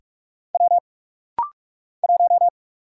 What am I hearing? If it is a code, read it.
Morse code, heard as a keyed tone, about 45 wpm, 700 Hz. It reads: W 1
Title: Single Letter-Number - Rapid-Fire 45wpm